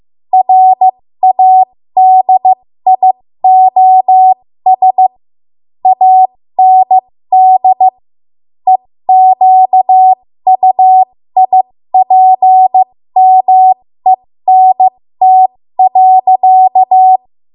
Morse code, heard as a keyed tone, about 15 wpm, spaced as for 13 wpm, 750 Hz. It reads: RADIOS AND EQUIPMENT.
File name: Morse / 13WPM